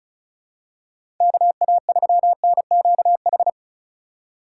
Morse code, heard as a keyed tone, about 34 wpm, 700 Hz. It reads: KA3NQH
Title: USA Call Signs - 6 Characters - 2x Character Spacing - 35wpm